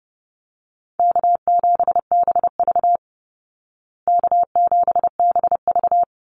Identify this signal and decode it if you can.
Morse code, heard as a keyed tone, about 30 wpm, 700 Hz. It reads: K7BV K7BV
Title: Call Signs Repeated 2x - 30wpm